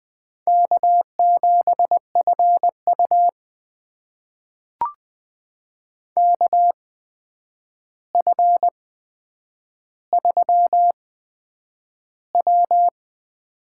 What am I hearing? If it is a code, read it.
Morse code, heard as a keyed tone, about 20 wpm, spaced as for 9 wpm, 700 Hz. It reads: K7FU KF3W
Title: USA Call Signs - 4 Characters - 8x Character Spacing - 20wpm